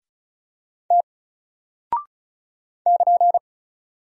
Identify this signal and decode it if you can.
Morse code, heard as a keyed tone, about 33 wpm, 700 Hz. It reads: T <KN>
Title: Single Character (Letter:Number:Punctuation:Prosign) 35wpm